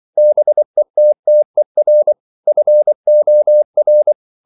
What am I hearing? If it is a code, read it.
Morse code, heard as a keyed tone, about 24 wpm, 600 Hz. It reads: BETTER FOR